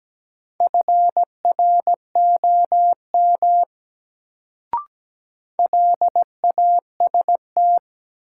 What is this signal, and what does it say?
Morse code, heard as a keyed tone, about 17 wpm, 700 Hz. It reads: FROM LAST
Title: Sets of 2 Words - New Word Drill - 17wpm